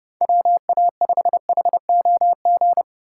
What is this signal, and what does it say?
Morse code, heard as a keyed tone, about 27 wpm, 700 Hz. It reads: WA5HOG